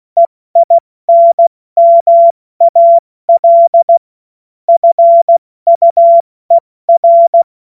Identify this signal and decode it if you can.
Morse code, heard as a keyed tone, about 16 wpm, spaced as for 14 wpm, 680 Hz. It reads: EINMAL FUER
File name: Morsekurs / Witze / Witz 21.November 2023 70 F80